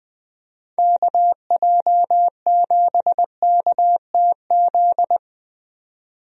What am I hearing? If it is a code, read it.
Morse code, heard as a keyed tone, about 21 wpm, 700 Hz. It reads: KJ7KTZ
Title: USA Call Signs - 6 Characters - 20wpm